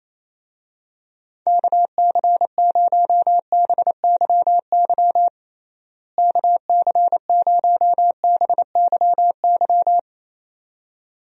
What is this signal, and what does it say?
Morse code, heard as a keyed tone, about 28 wpm, 700 Hz. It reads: KC0BYY KC0BYY
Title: USA Call Signs - 6 Characters - Repeated 2x - 28wpm